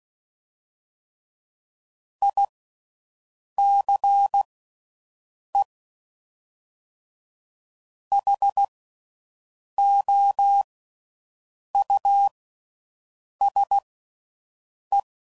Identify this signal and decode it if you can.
Morse code, heard as a keyed tone, about 16 wpm, spaced as for 6 wpm, 800 Hz. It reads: ICEHOUSE